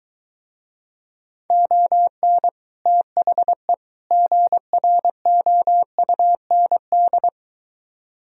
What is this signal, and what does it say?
Morse code, heard as a keyed tone, about 23 wpm, 700 Hz. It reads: ON THE GROUND